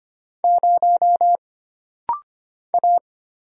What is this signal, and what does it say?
Morse code, heard as a keyed tone, about 25 wpm, 700 Hz. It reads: 0 A
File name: Single Character (Letter:Number:Punctuation:Prosign) - Rapid-Fire - 25wpm